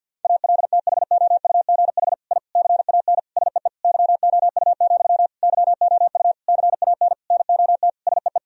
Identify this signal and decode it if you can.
Morse code, heard as a keyed tone, about 50 wpm, 700 Hz. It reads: ALTHOUGH I CAN SEE YOU, YOU CAN NOT SEE
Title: Sentences from Top 1000 Words - Review Part 3 - 50wpm